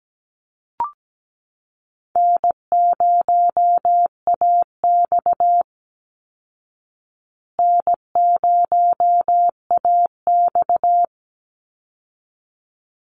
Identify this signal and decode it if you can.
Morse code, heard as a keyed tone, about 17 wpm, 700 Hz. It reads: N0AX N0AX
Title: Call Signs Repeated 3x - 17wpm